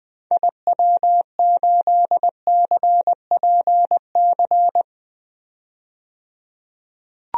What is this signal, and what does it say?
Morse code, heard as a keyed tone, about 20 wpm, 700 Hz. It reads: IW8CPC
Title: DXCC-100 Call Signs - 6 Characters - 20wpm